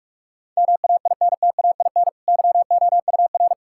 Morse code, heard as a keyed tone, about 46 wpm, 700 Hz. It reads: MAINTAIN YOUR
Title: Sentences from Top 1000 Words - Review Part 5 - 45wpm